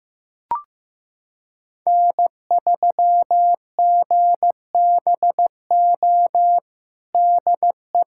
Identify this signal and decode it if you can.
Morse code, heard as a keyed tone, about 15 wpm, 700 Hz. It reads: N3GBO DE
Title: USA Callsign DE Callsign - 5 Characters - 15wpm